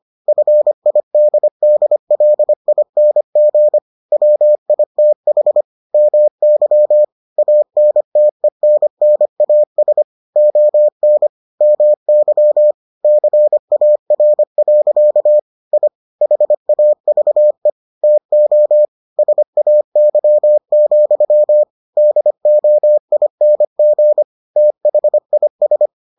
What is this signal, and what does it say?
Morse code, heard as a keyed tone, about 25 wpm, 600 Hz. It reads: FIDDLING WITH MY ANTENNAS ON MY CAR. I HAVE TO SAY, DOING THIS